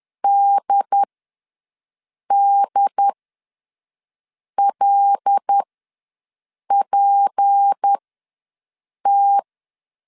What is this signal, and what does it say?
Morse code, heard as a keyed tone, about 10 wpm, spaced as for 5 wpm, 800 Hz. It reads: DDLPT
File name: LEZIONI / 6 PKX0 / 6 PKX0 ws 10wpm 4220